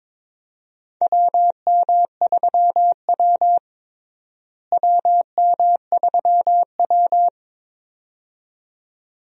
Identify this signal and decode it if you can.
Morse code, heard as a keyed tone, about 22 wpm, 700 Hz. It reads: WM3W WM3W
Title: USA Call Signs - 4 Characters - Repeated 2x - 22wpm